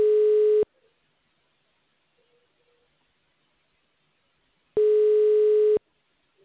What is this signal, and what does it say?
Ringback tone (Europe, 425 Hz).